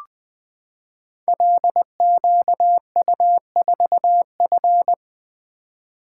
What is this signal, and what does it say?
Morse code, heard as a keyed tone, about 20 wpm, 700 Hz. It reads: LQU4F
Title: Code Groups - 5 Letter:Numbers - 20wpm